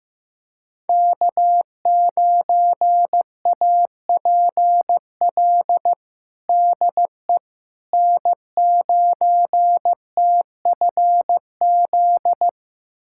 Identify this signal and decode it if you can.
Morse code, heard as a keyed tone, about 15 wpm, 700 Hz. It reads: K9APL DE N9TFZ
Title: USA Callsign DE Callsign - 5 Characters - 15wpm